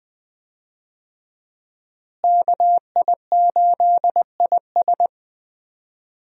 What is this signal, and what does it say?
Morse code, heard as a keyed tone, about 21 wpm, 700 Hz. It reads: KI8IS
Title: USA Call Signs - 5 Characters - Repeated 3x - 20wpm